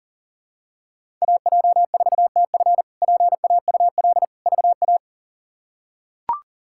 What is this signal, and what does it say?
Morse code, heard as a keyed tone, about 40 wpm, 690 Hz. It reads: AJ4TF PAUL VA
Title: K1USN SST QSO Practice - 40wpm